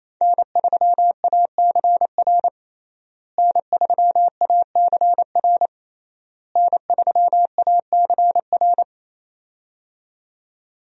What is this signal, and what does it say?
Morse code, heard as a keyed tone, about 28 wpm, 700 Hz. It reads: N3ACR N3ACR N3ACR
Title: USA Call Signs - 5 Characters - Repeated 3x - 28wpm